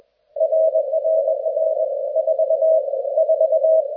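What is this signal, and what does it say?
Morse code, heard as a keyed tone, about 21 wpm, 640 Hz. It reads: RRR 4 4
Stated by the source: QSO between DL5AXI and RD3ARU during the RU-QRP Club "Russian Hunt" (Русская Охота) contest, with mechanical key and bug use mentioned.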